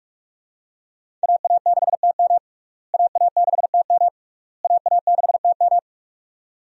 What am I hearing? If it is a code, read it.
Morse code, heard as a keyed tone, about 47 wpm, 700 Hz. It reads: AA6TM AA6TM AA6TM
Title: USA Call Signs - 5 Characters - Repeated 3x - 45wpm